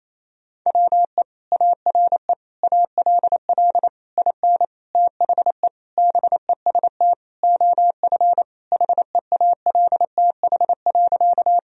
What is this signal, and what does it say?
Morse code, heard as a keyed tone, about 28 wpm, 700 Hz. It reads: WE ARE ALL IN THE BEST OF HEALTH.
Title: Sentences from Top 1000 Words - Review Part 3 - 28wpm